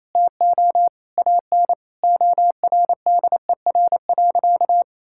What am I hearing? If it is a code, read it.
Morse code, heard as a keyed tone, about 28 wpm, 700 Hz. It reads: TO AN ORDER.
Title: Sentences from Top 1600 Words 28wpm